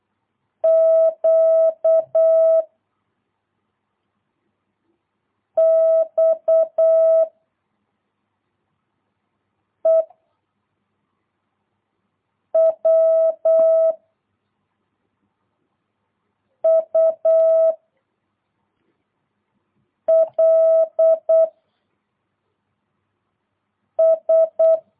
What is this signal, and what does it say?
Morse code, heard as a keyed tone, about 8 wpm, 660 Hz. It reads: Q X E W U L S
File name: jh8cbh / 2022cw